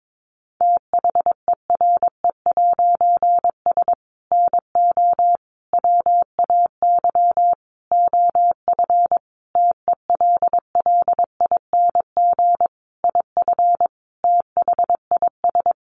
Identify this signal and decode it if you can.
Morse code, heard as a keyed tone, about 22 wpm, 700 Hz. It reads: THERE'S NO WAY OF TELLING IF THIS
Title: A Sound Of Thunder 22wpm